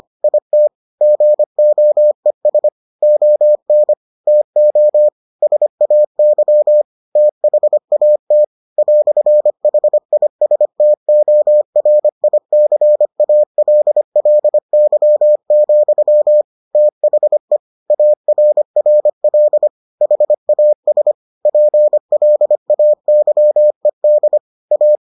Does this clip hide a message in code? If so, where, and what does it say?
Yes: Morse code from 0.2 to 24.9 s, heard as a keyed tone, about 25 wpm, 600 Hz. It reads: IT GOES ON TO SAY THAT "HISTORICALLY, THE ARRL HAS PLAYED A